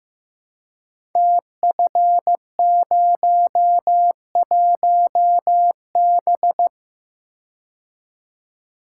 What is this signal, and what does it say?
Morse code, heard as a keyed tone, about 15 wpm, 700 Hz. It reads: TF01B
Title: Code Groups - 5 Letter:Numbers - 15wpm